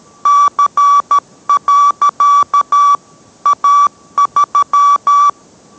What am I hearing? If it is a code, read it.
Morse code, heard as a keyed tone, about 13 wpm, 1150 Hz. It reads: C. A3